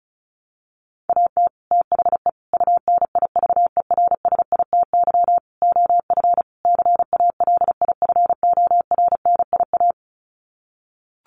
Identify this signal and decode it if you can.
Morse code, heard as a keyed tone, about 34 wpm, 700 Hz. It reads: AT THE UNIVERSITY OF CALIFORNIA
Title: Sets of 5 Words Encore2 35wpm